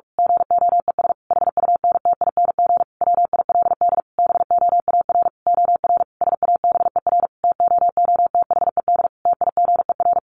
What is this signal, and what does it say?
Morse code, heard as a keyed tone, about 46 wpm, 700 Hz. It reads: GOES HUNTING WILD BOAR OR SABER TOOTHED TIGER
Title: A Sound Of Thunder 45wpm